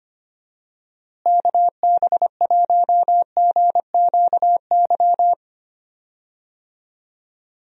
Morse code, heard as a keyed tone, about 25 wpm, 700 Hz. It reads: KB1GQY
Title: USA Call Signs - 6 Characters - Repeated 3x - 25wpm